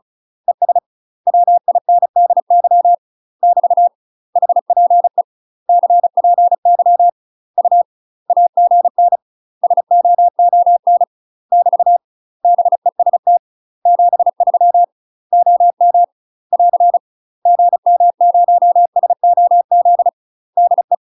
Morse code, heard as a keyed tone, about 36 wpm, 700 Hz. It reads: ES WINDY <BT> HPE CPY U AGN SOON <BT> BEST 73 OM <AR> GM0SOZ DE
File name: UK MASTER QSOs TRAINING / 35WPM / UK MASTER QSOs 0029